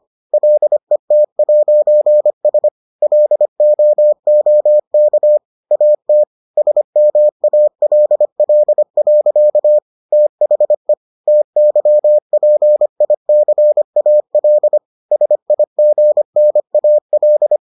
Morse code, heard as a keyed tone, about 25 wpm, 600 Hz. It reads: LET'S LOOK AT SMALL. THE TYPICAL SIGNAL